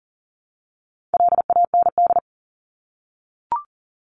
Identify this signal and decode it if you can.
Morse code, heard as a keyed tone, about 40 wpm, 700 Hz. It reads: LAND
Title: Top 500 Words 40wpm